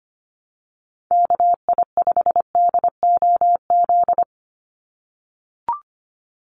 Morse code, heard as a keyed tone, about 25 wpm, 700 Hz. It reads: KI5DOZ
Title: USA Call Signs 25wpm